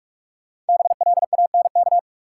Morse code, heard as a keyed tone, about 46 wpm, 700 Hz. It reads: BLANK
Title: Sentences from Top 1500 Words - New Word Drill - 45wpm